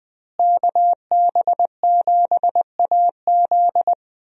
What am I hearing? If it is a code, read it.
Morse code, heard as a keyed tone, about 20 wpm, 700 Hz. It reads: KB7AZ
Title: USA Call Signs - 5 Characters - Repeated 3x - 20wpm